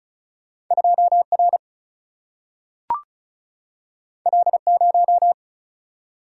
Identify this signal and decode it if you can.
Morse code, heard as a keyed tone, about 36 wpm, 700 Hz. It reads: 2 R L 0
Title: Single-Letter Single-Number 35wpm